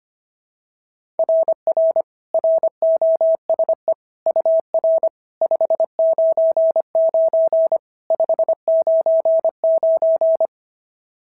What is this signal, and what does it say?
Morse code, heard as a keyed tone, about 25 wpm, 690 Hz. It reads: RR ROSE UR 599 599
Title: SOTA QSO Practice 25wpm